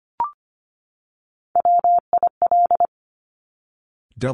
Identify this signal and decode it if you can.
Morse code, heard as a keyed tone, about 25 wpm, 700 Hz. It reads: WIL
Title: Top Three Letter Combinations in Words 25wpm